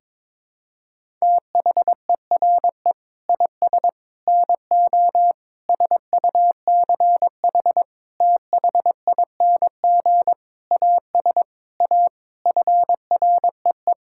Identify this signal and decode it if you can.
Morse code, heard as a keyed tone, about 22 wpm, 700 Hz. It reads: THERE IS NO SUCH THING AS A FREE
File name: Sets of 9 Words - 22wpm